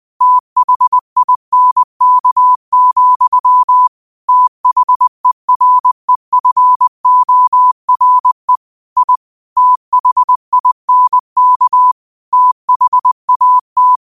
Morse code, heard as a keyed tone, about 20 wpm, 1000 Hz. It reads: THINK, THEREFORE I THINK THAT